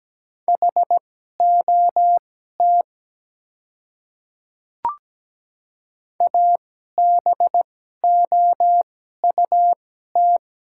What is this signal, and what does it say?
Morse code, heard as a keyed tone, about 17 wpm, spaced as for 12 wpm, 700 Hz. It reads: HOT ABOUT
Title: 2x Character Spacing - Top 300 Words - 17wpm